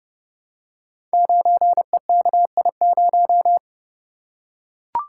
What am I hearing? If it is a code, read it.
Morse code, heard as a keyed tone, about 30 wpm, 700 Hz. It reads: 9EKI0
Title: Code Groups - 5 Letter:Numbers - 30wpm